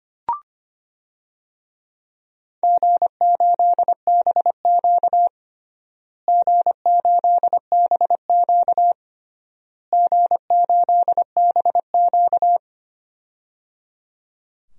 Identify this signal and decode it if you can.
Morse code, heard as a keyed tone, about 25 wpm, 700 Hz. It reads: G8BQ G8BQ G8BQ
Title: DXCC-100 Call Signs - 4 Characters - Repeated 3x - 25wpm